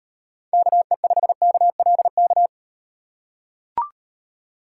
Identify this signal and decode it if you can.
Morse code, heard as a keyed tone, about 36 wpm, 700 Hz. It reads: KE5KLK